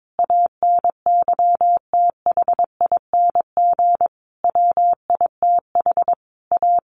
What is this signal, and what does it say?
Morse code, heard as a keyed tone, about 22 wpm, 700 Hz. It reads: ANYTHING WITH A
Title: Sentences from Top 1200 Words 22wpm